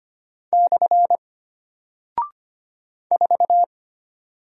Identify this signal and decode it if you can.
Morse code, heard as a keyed tone, about 25 wpm, 700 Hz. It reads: / 4
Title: Single Character (Letter:Number:Punctuation:Prosign) Alt 25wpm